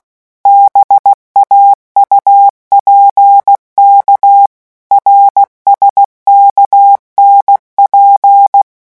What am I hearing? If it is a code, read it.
Morse code, heard as a keyed tone, about 16 wpm, 800 Hz. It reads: BAUPK RSKNP
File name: morse mp3 / revision5-vitesse 15 mots